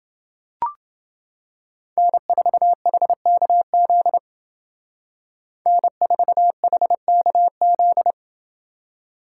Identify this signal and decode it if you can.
Morse code, heard as a keyed tone, about 30 wpm, 700 Hz. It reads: N4HKZ N4HKZ